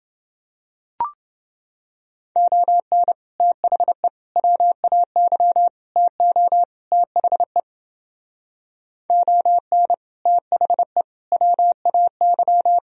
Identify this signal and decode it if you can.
Morse code, heard as a keyed tone, about 30 wpm, 700 Hz. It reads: ON THE WAY TO THE ON THE WAY